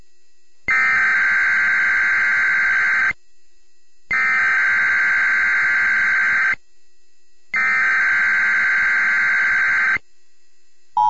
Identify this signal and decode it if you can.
Emergency Alert System (SAME) tones (header bursts), heard as harsh buzzing bursts. It is heard from 0.7 to 10.0 s.